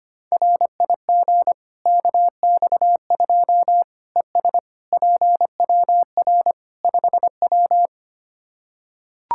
Morse code, heard as a keyed tone, about 25 wpm, 700 Hz. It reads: RIG KX2 ES PWR 5W
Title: K1USN SST QSO Practice 25wpm